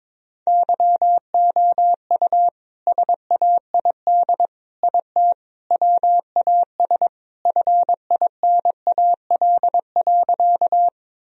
Morse code, heard as a keyed tone, about 22 wpm, 700 Hz. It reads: YOU SAID IT WAS FINAL.
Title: Sentences from Top 1000 Words - Review Part 2 - 22wpm